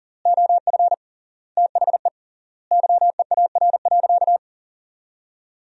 Morse code, heard as a keyed tone, about 40 wpm, 700 Hz. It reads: OF THE YEAR.
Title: 3x Word-Spacing - Sentences from Top 300 Words - 40wpm